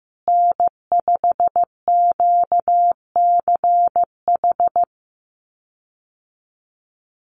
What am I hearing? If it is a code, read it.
Morse code, heard as a keyed tone, about 15 wpm, 700 Hz. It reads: N5QCH
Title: USA Call Signs Repeated 3x 15wpm